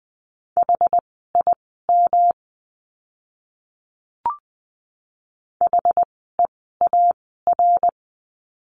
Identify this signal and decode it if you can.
Morse code, heard as a keyed tone, about 20 wpm, spaced as for 14 wpm, 700 Hz. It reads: HIM HEAR
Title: 206 Lesson 33f - Letter j - Words 20wpm